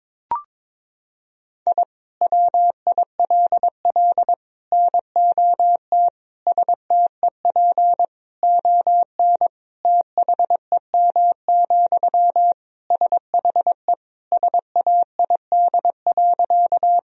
Morse code, heard as a keyed tone, about 22 wpm, 700 Hz. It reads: I WILL NOT STEP ON THEM, SHE SAID.
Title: Sentences from Top 1000 Words - Review Part 2 - 22wpm